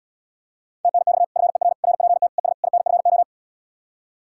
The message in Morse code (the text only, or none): TNX FER RPT ES INFO